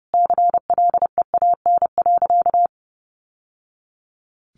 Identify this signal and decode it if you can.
Morse code, heard as a keyed tone, about 30 wpm, 700 Hz. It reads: CLEAN.
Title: Sentences from Top 1100 Words 30wpm